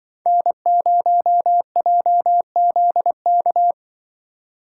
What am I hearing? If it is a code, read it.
Morse code, heard as a keyed tone, about 21 wpm, 700 Hz. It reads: N0JZK
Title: Speed Racing - USA 1x3 Call Signs 20wpm